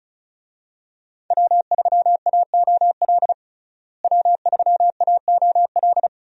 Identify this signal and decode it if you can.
Morse code, heard as a keyed tone, about 36 wpm, 700 Hz. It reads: W3AOL W3AOL
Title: USA Call Signs - 5 Characters - Repeated 2x - 35wpm